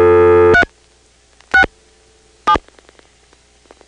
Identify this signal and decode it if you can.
DTMF, heard as telephone touch tones, tones of about 90 ms. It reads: 3 6 7
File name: phonetrips / PrinceWmCoVA5HQ